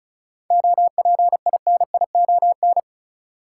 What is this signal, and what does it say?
Morse code, heard as a keyed tone, about 36 wpm, 700 Hz. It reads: OPINION
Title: Sentences from Top 1200 Words - New Word Drill - 35wpm